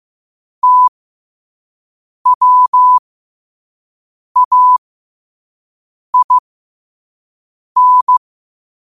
Morse code, heard as a keyed tone, about 15 wpm, spaced as for 5 wpm, 1000 Hz. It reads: TWAIN
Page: New quotes every day in morse code at 5 Words per minute.